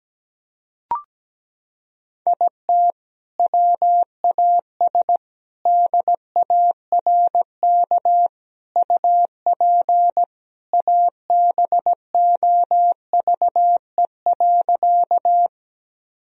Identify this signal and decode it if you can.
Morse code, heard as a keyed tone, about 17 wpm, 700 Hz. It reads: IT WAS DARK UP ABOVE.
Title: Sentences from Top 1000 Words - Review Part 2 - 17wpm